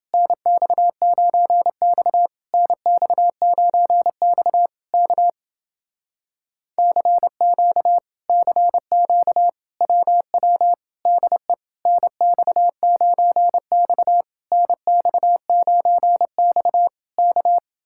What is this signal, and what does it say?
Morse code, heard as a keyed tone, about 29 wpm, 700 Hz. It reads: NX9X NX9X K CQ CQ WW DE NX9X NX9X K